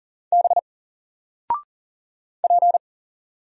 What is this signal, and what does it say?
Morse code, heard as a keyed tone, about 40 wpm, 700 Hz. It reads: B P
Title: Single Character (Letter:Number:Punctuation:Prosign) Alt 40wpm